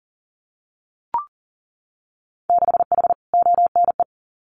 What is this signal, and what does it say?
Morse code, heard as a keyed tone, about 42 wpm, 700 Hz. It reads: 6H ONE
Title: Winter Field Day Exchanges Encore 40wpm